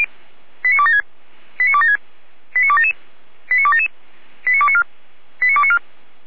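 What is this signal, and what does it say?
Five-tone selective call (ZVEI), heard as quick runs of five tones, tones of about 70 ms. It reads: 89276 89276 89277 89277 89294 89294
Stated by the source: ZVEI five-tone call 89265 received on SDR.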